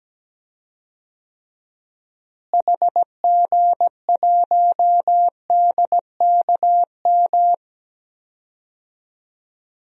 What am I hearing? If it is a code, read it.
Morse code, heard as a keyed tone, about 17 wpm, 700 Hz. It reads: HG1DKM
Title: DXCC-100 Call Signs - 6 Characters - Repeated 3x - 17wpm